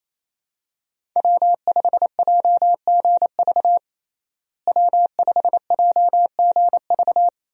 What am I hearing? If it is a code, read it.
Morse code, heard as a keyed tone, about 27 wpm, 700 Hz. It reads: W5JGV W5JGV
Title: USA Call Signs - 5 Characters - Repeated 2x - 28wpm